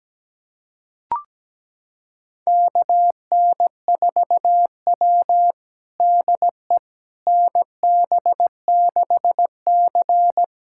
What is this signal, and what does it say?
Morse code, heard as a keyed tone, about 17 wpm, 700 Hz. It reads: KN4W DE NB6C
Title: USA Callsign DE Callsign - 4 Characters - 17wpm